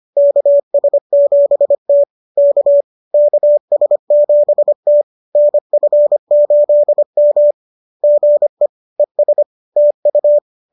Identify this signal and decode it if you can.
Morse code, heard as a keyed tone, about 25 wpm, 590 Hz. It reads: KS7T K KS7T NF8M GE ES TU